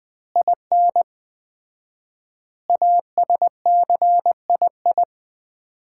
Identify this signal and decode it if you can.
Morse code, heard as a keyed tone, about 20 wpm, 700 Hz. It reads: IN ASCII
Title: 4x Word-Spacing - Sets of 5 Words - 20wpm